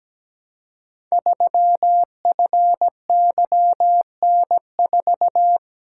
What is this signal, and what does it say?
Morse code, heard as a keyed tone, about 17 wpm, 700 Hz. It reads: 3FYN4
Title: Code Groups - 5 Letter:Numbers - 17wpm